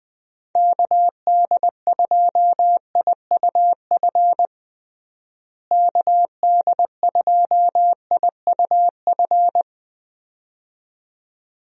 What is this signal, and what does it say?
Morse code, heard as a keyed tone, about 20 wpm, 700 Hz. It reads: KD2IUF KD2IUF
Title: USA Call Signs - 6 Characters - Repeated 2x - 20wpm